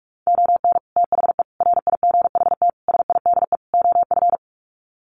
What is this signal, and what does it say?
Morse code, heard as a keyed tone, about 44 wpm, 700 Hz. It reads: ON THE RIGHT SIDE OF
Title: Sets of 5 Words Encore 45wpm